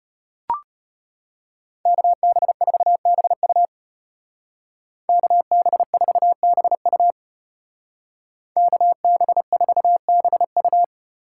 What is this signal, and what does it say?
Morse code, heard as a keyed tone, about 31 wpm, 700 Hz. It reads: KB4BU KB4BU KB4BU
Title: Speed Racing - USA 2x2 Call Signs 25wpm